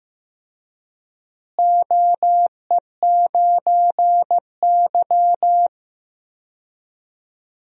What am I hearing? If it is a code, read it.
Morse code, heard as a keyed tone, about 15 wpm, 700 Hz. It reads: OE9Y
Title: DXCC-100 Call Signs - 4 Characters - Repeated 2x - 15wpm